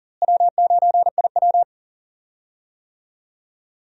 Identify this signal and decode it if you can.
Morse code, heard as a keyed tone, about 40 wpm, 700 Hz. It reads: W9IW
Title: USA Call Signs - 4 Characters - Repeated 2x - 40wpm